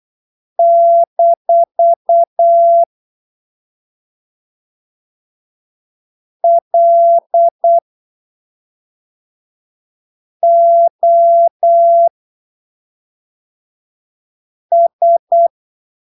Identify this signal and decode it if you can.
Morse code, heard as a keyed tone, about 8 wpm, spaced as for 3 wpm, 680 Hz. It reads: -LOS
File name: arquivos / texto / texto 15 08ppm